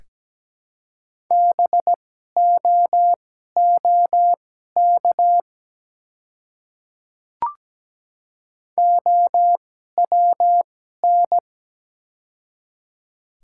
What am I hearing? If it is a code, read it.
Morse code, heard as a keyed tone, about 17 wpm, spaced as for 12 wpm, 700 Hz. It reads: BOOK OWN
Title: 2x Character Spacing - Top 300 Words - 17wpm